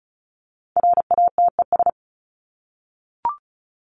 Morse code, heard as a keyed tone, about 34 wpm, 700 Hz. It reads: RATES
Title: ES:S Suffix - Top 100 Words with Base Words 35wpm